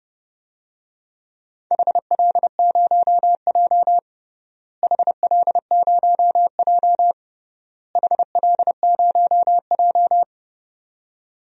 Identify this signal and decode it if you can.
Morse code, heard as a keyed tone, about 28 wpm, 700 Hz. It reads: HL0J HL0J HL0J
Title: DXCC-100 Call Signs - 4 Characters - Repeated 3x - 30wpm